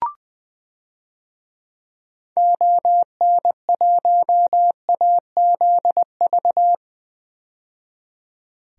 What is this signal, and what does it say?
Morse code, heard as a keyed tone, about 20 wpm, 700 Hz. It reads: ON1AZV
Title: DXCC-100 Call Signs - 6 Characters - 20wpm